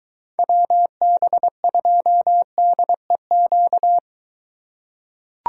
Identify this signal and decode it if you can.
Morse code, heard as a keyed tone, about 21 wpm, 700 Hz. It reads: WB2DEQ